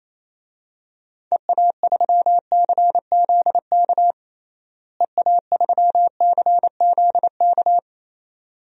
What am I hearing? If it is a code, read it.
Morse code, heard as a keyed tone, about 28 wpm, 700 Hz. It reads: EA3CZK EA3CZK
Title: DXCC-100 Call Signs - 6 Characters - Repeated 2x - 28wpm